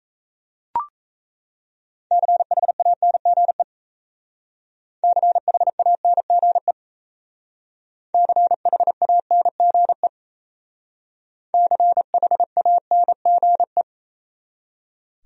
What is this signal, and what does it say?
Morse code, heard as a keyed tone, about 34 wpm, 700 Hz. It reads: CHANGE CHANGE CHANGE CHANGE
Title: Speed-Racing Top 200 Words 28wpm